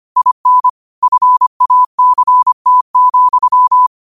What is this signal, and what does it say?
Morse code, heard as a keyed tone, about 25 wpm, 1000 Hz. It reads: IN FACT,
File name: qotdfiles / 25wpm